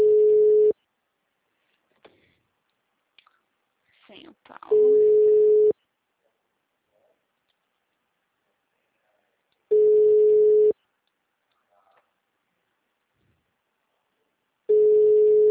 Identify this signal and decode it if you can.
Ringback tone (Europe, 425 Hz), with speech.